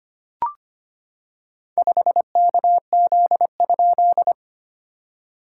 Morse code, heard as a keyed tone, about 25 wpm, 690 Hz. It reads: 5KZ?
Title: SOTA QSO Practice 25wpm